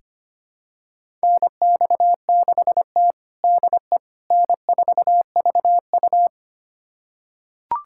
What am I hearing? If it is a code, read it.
Morse code, heard as a keyed tone, about 25 wpm, 700 Hz. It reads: NX6T DE N4VU
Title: USA Callsign DE Callsign - 4 Characters - 25wpm